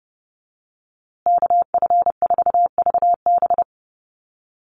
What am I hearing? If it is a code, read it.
Morse code, heard as a keyed tone, about 29 wpm, 700 Hz. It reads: KF4VB
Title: USA Call Signs Repeated 2x 30wpm